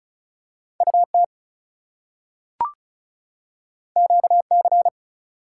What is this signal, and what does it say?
Morse code, heard as a keyed tone, about 36 wpm, 700 Hz. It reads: UT QC
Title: US States and Canadian Provinces 35wpm